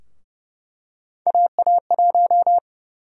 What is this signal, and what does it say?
Morse code, heard as a keyed tone, about 30 wpm, 700 Hz. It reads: AA1
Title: USA Call Sign Prefixes - 30wpm